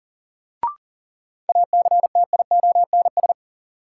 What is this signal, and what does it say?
Morse code, heard as a keyed tone, about 40 wpm, 700 Hz. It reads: ACTIONS
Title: Sentences from Top 1000 Words - Review Part 6 - New Word Drill - 40wpm